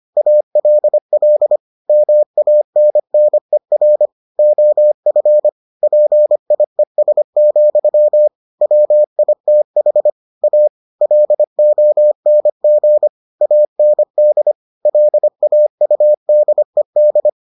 Morse code, heard as a keyed tone, about 25 wpm, 600 Hz. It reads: ALL MANNER OF PIES, WITH A LONG AND LAUDED